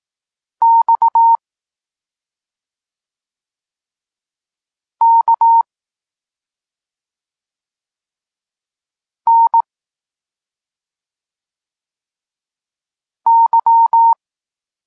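Morse code, heard as a keyed tone, about 18 wpm, 920 Hz. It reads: X K N Y
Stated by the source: Polish Morse lesson teaching the letter C, from SP9MOA Niepołomicki Klub Krótkofalowców.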